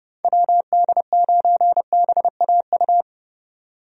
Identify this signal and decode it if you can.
Morse code, heard as a keyed tone, about 30 wpm, 700 Hz. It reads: WD9BAU
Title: USA Call Signs - 6 Characters - 30wpm